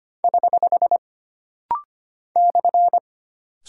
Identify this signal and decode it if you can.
Morse code, heard as a keyed tone, about 25 wpm, 700 Hz. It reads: <HH> /
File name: Single Punctuation-Prosign - Rapid-Fire - 25wpm